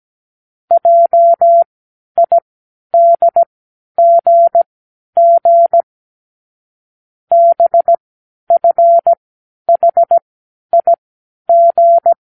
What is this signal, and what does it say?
Morse code, heard as a keyed tone, about 17 wpm, spaced as for 11 wpm, 680 Hz. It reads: JIDGG BFHIG
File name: k7qo / Lesson15(A-J)